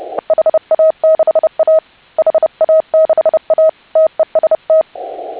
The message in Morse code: HA6A HA6A TEST

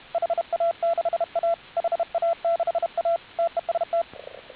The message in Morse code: HA6A HA6A TEST

